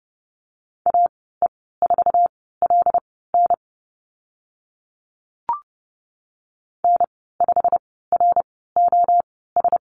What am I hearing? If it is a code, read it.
Morse code, heard as a keyed tone, about 30 wpm, spaced as for 17 wpm, 700 Hz. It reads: AE4LN N5ROS
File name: Lesson 15g - Number 5 - Call Signs 30wpm